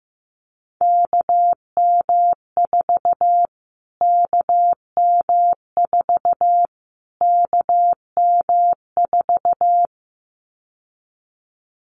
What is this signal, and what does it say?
Morse code, heard as a keyed tone, about 15 wpm, 700 Hz. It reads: KM4 KM4 KM4
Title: Callsign Prefixes for Top 25 DXCC - Repeated 3x - 15wpm